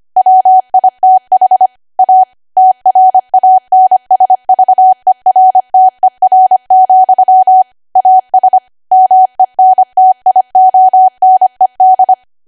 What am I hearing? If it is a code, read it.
Morse code, heard as a keyed tone, about 25 wpm, 750 Hz. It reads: WITH A TRANSVERTER, AS MENTIONED
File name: Archive / 25 WPM / 25WPM